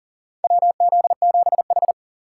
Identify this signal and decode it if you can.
Morse code, heard as a keyed tone, about 40 wpm, 700 Hz. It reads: WZ7H
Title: USA Call Signs - 4 Characters - 3x Character Spacing - 40wpm